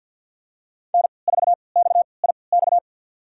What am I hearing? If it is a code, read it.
Morse code, heard as a keyed tone, about 51 wpm, spaced as for 28 wpm, 700 Hz. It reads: N4XIX
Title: USA Call Signs - 5 Characters - 3x Character Spacing - 50wpm